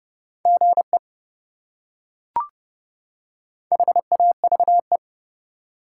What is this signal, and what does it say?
Morse code, heard as a keyed tone, about 30 wpm, 700 Hz. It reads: GE HAVE
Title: Top 180 QSO Elements 30wpm